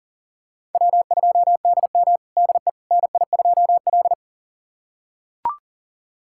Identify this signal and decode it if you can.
Morse code, heard as a keyed tone, about 37 wpm, 700 Hz. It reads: W2DM DE NI2L
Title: USA Callsign DE Callsign - 4 Characters - 40wpm